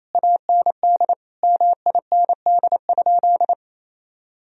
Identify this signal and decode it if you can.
Morse code, heard as a keyed tone, about 27 wpm, 700 Hz. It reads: AND MIND?
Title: Sentences from Top 1000 Words - Review Part 2 - 28wpm